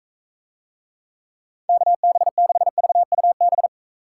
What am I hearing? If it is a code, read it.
Morse code, heard as a keyed tone, about 44 wpm, 700 Hz. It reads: KB6VUB